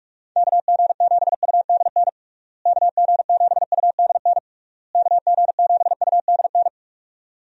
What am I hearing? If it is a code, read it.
Morse code, heard as a keyed tone, about 42 wpm, 700 Hz. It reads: KG7UDN KG7UDN KG7UDN
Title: USA Call Signs - 6 Characters - Repeated 3x - 45wpm